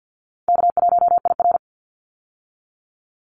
Morse code, heard as a keyed tone, about 50 wpm, 700 Hz. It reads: K1IR
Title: Call Signs Repeated 3x - 50wpm